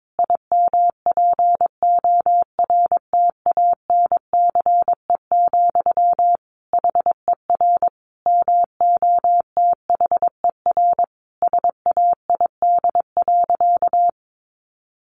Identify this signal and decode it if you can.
Morse code, heard as a keyed tone, about 22 wpm, 700 Hz. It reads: IMPORTANCE, HER MOTHER SAID.
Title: Sentences from Top 1500 Words 22wpm